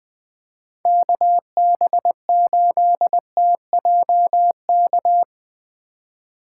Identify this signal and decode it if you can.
Morse code, heard as a keyed tone, about 21 wpm, 700 Hz. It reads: KB8TJK
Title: USA Call Signs - 6 Characters - 2x Character Spacing - 20wpm